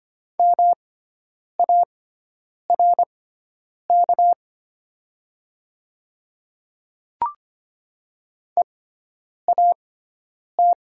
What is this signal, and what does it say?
Morse code, heard as a keyed tone, about 25 wpm, spaced as for 9 wpm, 700 Hz. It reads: MARK EAT
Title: 6x Character Spacing - Top 300 Words - 25wpm